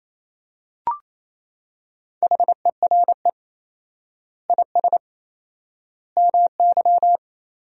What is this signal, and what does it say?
Morse code, heard as a keyed tone, about 27 wpm, 700 Hz. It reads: HERE IS MY
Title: 4x Word-Spacing - Sentences from Top 200 Words - 28wpm